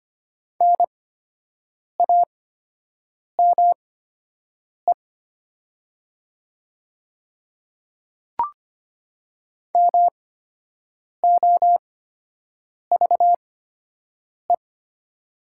Morse code, heard as a keyed tone, about 25 wpm, spaced as for 7 wpm, 700 Hz. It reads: NAME MOVE
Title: 8x Character Spacing - Top 300 Words - 25wpm